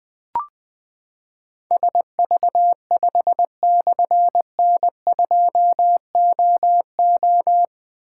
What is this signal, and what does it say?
Morse code, heard as a keyed tone, about 20 wpm, 700 Hz. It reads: SV5/N2OO
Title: Difficult Call Signs 20wpm